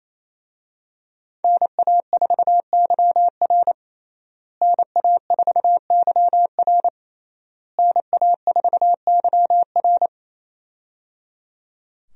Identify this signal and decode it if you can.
Morse code, heard as a keyed tone, about 28 wpm, 700 Hz. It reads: NA4YR NA4YR NA4YR
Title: USA Call Signs - 5 Characters - Repeated 3x - 28wpm